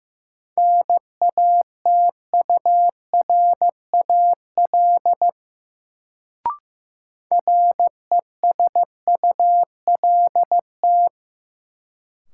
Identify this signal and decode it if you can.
Morse code, heard as a keyed tone, about 16 wpm, 700 Hz. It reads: NATURAL RESULT
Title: Sentences from Top 700 Words - New Word Drill - 15wpm